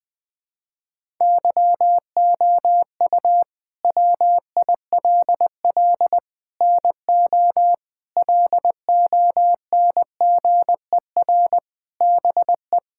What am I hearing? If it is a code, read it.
Morse code, heard as a keyed tone, about 20 wpm, 700 Hz. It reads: YOU WILL NO LONGER BE
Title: Sentences with 6-Word Phrases - Spoken Beginning - 20wpm